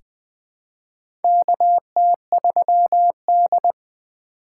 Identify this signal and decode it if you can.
Morse code, heard as a keyed tone, about 21 wpm, 700 Hz. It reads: KT3D
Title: USA Call Signs - 4 Characters - 4x Character Spacing - 20wpm